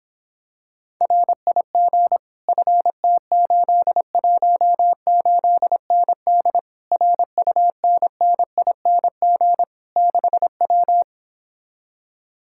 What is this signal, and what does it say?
Morse code, heard as a keyed tone, about 26 wpm, 700 Hz. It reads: RIG FT818ND RUNNING 6W